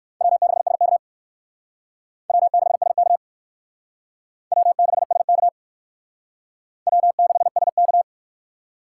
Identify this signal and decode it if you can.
Morse code, heard as a keyed tone, about 56 wpm, 700 Hz. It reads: W6SK W6SK W6SK W6SK
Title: Speed Racing - USA 1x2 Call Signs 45wpm